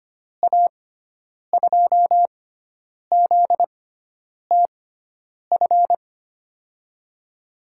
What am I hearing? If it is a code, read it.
Morse code, heard as a keyed tone, about 25 wpm, spaced as for 9 wpm, 700 Hz. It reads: A2ZTF
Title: USA Call Signs - 6x Character Spacing - 25wpm